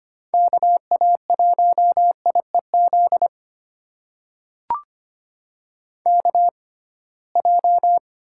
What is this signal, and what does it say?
Morse code, heard as a keyed tone, about 25 wpm, spaced as for 19 wpm, 700 Hz. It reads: KA1IEZ KJ
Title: USA Call Signs - 6 Characters - 6x Character Spacing - 25wpm